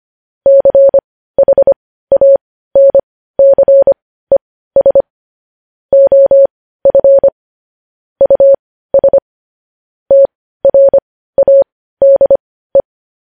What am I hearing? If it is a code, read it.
Morse code, heard as a keyed tone, about 25 wpm, spaced as for 15 wpm, 550 Hz. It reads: CHANCES OF US TRADE
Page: News Headlines in Morse Code at 15 WPM